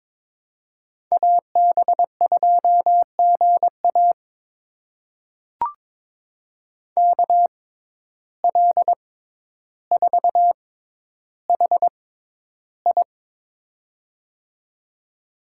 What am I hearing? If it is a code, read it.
Morse code, heard as a keyed tone, about 21 wpm, spaced as for 11 wpm, 700 Hz. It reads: AB2GA KL4HI
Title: USA Call Signs - 5 Characters - 6x Character Spacing - 22wpm